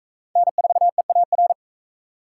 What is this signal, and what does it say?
Morse code, heard as a keyed tone, about 40 wpm, 700 Hz. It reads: N4EAR